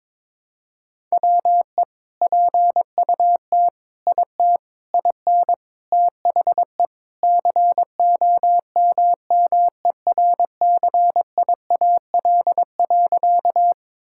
Morse code, heard as a keyed tone, about 22 wpm, 700 Hz. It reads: WE PUT IT IN THE COMMERCIAL.
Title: Sentences from Top 1000 Words - Review Part 5 - 22wpm